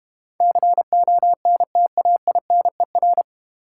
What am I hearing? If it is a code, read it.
Morse code, heard as a keyed tone, about 31 wpm, 700 Hz. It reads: CONTAINER
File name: Easy Words - Speed-Racing with Hints - 25wpm